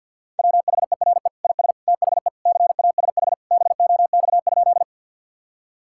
Morse code, heard as a keyed tone, about 51 wpm, 700 Hz. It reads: WHERE IS THE CASH BOX?
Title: Sentences from Top 1000 Words - Review Part 5 - 50wpm